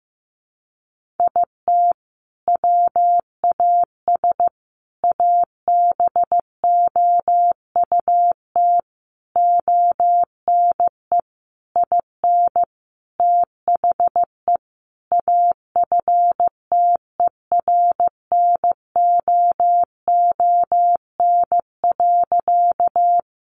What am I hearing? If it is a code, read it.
Morse code, heard as a keyed tone, about 15 wpm, 700 Hz. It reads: IT WAS ABOUT ONE IN THE AFTERNOON.
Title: Sentences from Top 1500 Words 15wpm